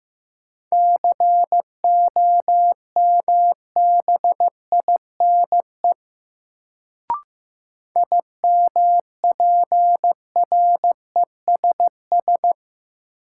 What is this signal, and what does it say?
Morse code, heard as a keyed tone, about 15 wpm, 700 Hz. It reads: COMBINE IMPRESS
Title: Sentences from Top 1600 Words - New Word Drill Alt - 15wpm